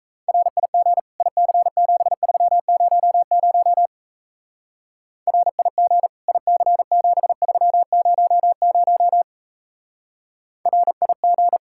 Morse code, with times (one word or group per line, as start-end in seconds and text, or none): 0.3-1.0 RIG
1.2-3.9 IC7300
5.3-6.1 RIG
6.3-9.2 IC7300
10.7-11.6 RIG